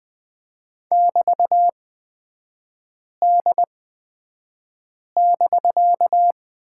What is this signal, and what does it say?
Morse code, heard as a keyed tone, about 20 wpm, spaced as for 5 wpm, 700 Hz. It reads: <BT>D<BK>
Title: 250 Lesson 40e - BK - Three Characters 20wpm